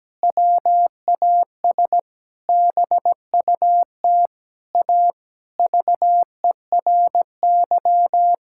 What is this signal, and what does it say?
Morse code, heard as a keyed tone, about 17 wpm, 700 Hz. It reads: WAS BUT A VERY
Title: Sentences from Top 1600 Words 17wpm